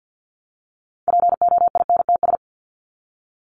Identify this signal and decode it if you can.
Morse code, heard as a keyed tone, about 50 wpm, 700 Hz. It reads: POINTS
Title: ES:S Suffix - Top 100 Words with Base Words 50wpm